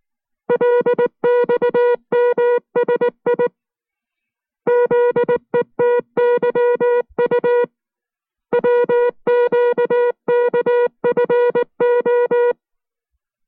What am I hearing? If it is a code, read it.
Morse code, heard as a keyed tone, about 19 wpm, 480 Hz. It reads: LXMSI ZETYU WQKFO